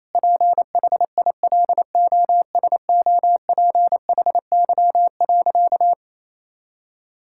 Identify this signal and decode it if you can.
Morse code, heard as a keyed tone, about 28 wpm, 700 Hz. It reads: PHILOSOPHY.
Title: Sentences from Top 1000 Words - Review Part 6 - 28wpm